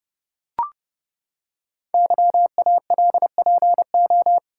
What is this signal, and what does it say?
Morse code, heard as a keyed tone, about 30 wpm, 700 Hz. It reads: YALPO
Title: Code Groups - 5 Letters - 30wpm